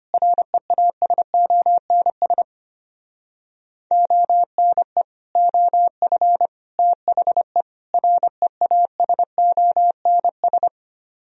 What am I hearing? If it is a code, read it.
Morse code, heard as a keyed tone, about 32 wpm, 700 Hz. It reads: REASONS ONE OF THE REASONS